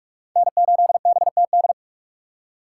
Morse code, heard as a keyed tone, about 46 wpm, 700 Hz. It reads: N8BTD
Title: USA Call Signs - 5 Characters - 6x Character Spacing - 45wpm